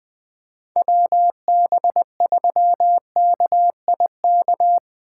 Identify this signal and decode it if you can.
Morse code, heard as a keyed tone, about 20 wpm, 700 Hz. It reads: WB3KIK
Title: USA Call Signs - 6 Characters - 4x Character Spacing - 20wpm